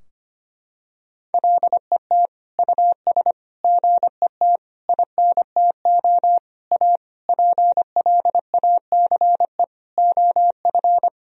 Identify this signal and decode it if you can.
Morse code, heard as a keyed tone, about 25 wpm, 700 Hz. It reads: LET US GET INTO A PLACE OF
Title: Sentences from Top 1000 Words - Review Part 4 - 25wpm